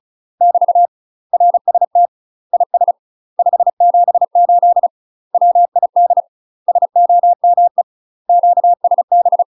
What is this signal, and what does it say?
Morse code, heard as a keyed tone, about 36 wpm, 700 Hz. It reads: <BT> RST IS 578 WID SOME QSB
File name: UK MASTER QSOs TRAINING / 35WPM / UK MASTER QSOs 0006